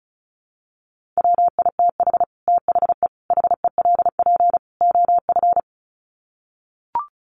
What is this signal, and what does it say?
Morse code, heard as a keyed tone, about 34 wpm, 700 Hz. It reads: WITH THE HELP OF
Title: Sets of 4 Words 35wpm